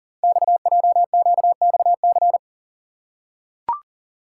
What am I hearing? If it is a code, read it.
Morse code, heard as a keyed tone, about 40 wpm, 690 Hz. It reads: XJQXC
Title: Code Groups - 5 Letters - 40wpm